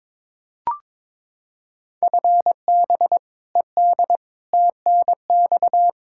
Morse code, heard as a keyed tone, about 22 wpm, 700 Hz. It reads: FB ED TNX
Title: Practice QSO - Exchange 3 - More Info - 22wpm